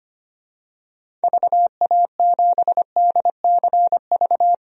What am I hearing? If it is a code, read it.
Morse code, heard as a keyed tone, about 25 wpm, 700 Hz. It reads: VA7DCV
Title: DXCC-100 Call Signs - 6 Characters - 25wpm